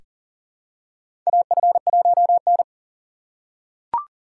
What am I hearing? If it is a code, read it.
Morse code, heard as a keyed tone, about 40 wpm, 700 Hz. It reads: AF1N
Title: USA Call Signs - 3x Character Spacing - 40wpm